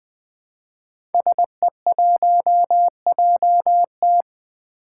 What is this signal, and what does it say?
Morse code, heard as a keyed tone, about 20 wpm, 700 Hz. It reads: SE1JT
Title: Five Challenging Letter-Number Combinations 20wpm